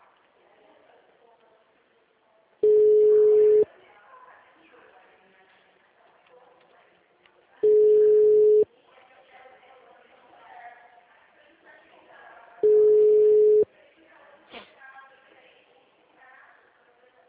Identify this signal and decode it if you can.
Ringback tone (Europe, 425 Hz), with speech.